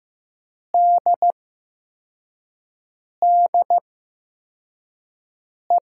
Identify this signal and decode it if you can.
Morse code, heard as a keyed tone, about 15 wpm, 700 Hz. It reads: D D E
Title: USA Call Signs - 8x Character Spacing - 15wpm